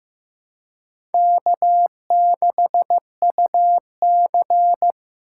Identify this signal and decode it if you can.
Morse code, heard as a keyed tone, about 15 wpm, 700 Hz. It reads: K6UC
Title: USA Call Signs - 3x Character Spacing - 15wpm